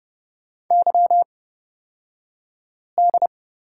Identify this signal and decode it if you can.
Morse code, heard as a keyed tone, about 31 wpm, spaced as for 5 wpm, 700 Hz. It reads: YD
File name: Lesson 36d - BT - Two Characters 30wpm